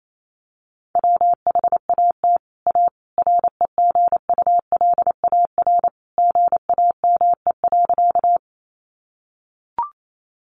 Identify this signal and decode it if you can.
Morse code, heard as a keyed tone, about 28 wpm, 700 Hz. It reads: WHAT A REGULAR GAME.
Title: Sentences from Top 900 Words 28wpm